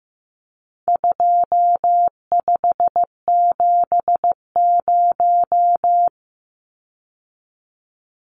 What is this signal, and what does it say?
Morse code, heard as a keyed tone, about 14 wpm, 700 Hz. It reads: 2570
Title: Numbers - 4-Digit 15wpm